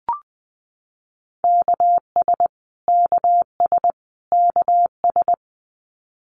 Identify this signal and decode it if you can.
Morse code, heard as a keyed tone, about 20 wpm, 700 Hz. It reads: KS KS KS
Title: US States Repeated 3x 20wpm